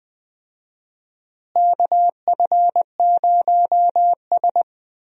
Morse code, heard as a keyed tone, about 21 wpm, 700 Hz. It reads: KF0S
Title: USA Call Signs - 4 Characters - Repeated 3x - 20wpm